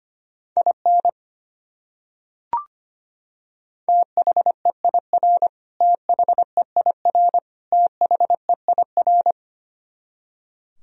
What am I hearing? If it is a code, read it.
Morse code, heard as a keyed tone, about 25 wpm, 700 Hz. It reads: IN THEIR THEIR THEIR
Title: Top 100 Words Repeated 25wpm